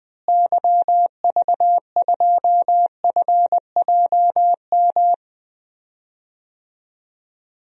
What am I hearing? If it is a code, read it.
Morse code, heard as a keyed tone, about 20 wpm, 700 Hz. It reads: YV2FJM
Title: DXCC-100 Call Signs - 6 Characters - Repeated 2x - 20wpm